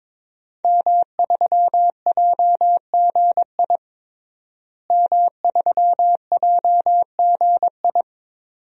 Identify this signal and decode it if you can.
Morse code, heard as a keyed tone, about 22 wpm, 700 Hz. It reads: M3JGI M3JGI
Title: DXCC-100 Call Signs - 5 Characters - Repeated 2x - 22wpm